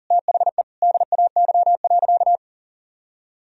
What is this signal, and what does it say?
Morse code, heard as a keyed tone, about 40 wpm, 700 Hz. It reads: THE DAY.
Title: Sentences from Top 1000 Words - Review Part 3 - 40wpm